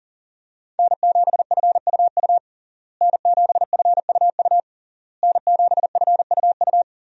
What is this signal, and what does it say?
Morse code, heard as a keyed tone, about 40 wpm, 700 Hz. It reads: N7FUU N7FUU N7FUU
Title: USA Call Signs - 5 Characters - Repeated 3x - 40wpm